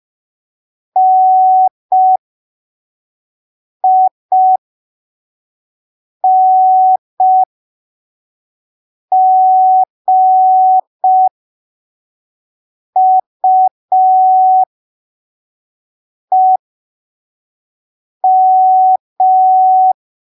Morse code, heard as a keyed tone, about 5 wpm, spaced as for 3 wpm, 750 Hz. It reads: NINGUEM